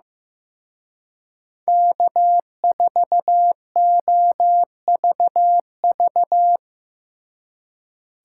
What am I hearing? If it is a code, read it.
Morse code, heard as a keyed tone, about 17 wpm, 700 Hz. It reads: K4OVV
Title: Speed Racing - USA 1x3 Call Signs 15wpm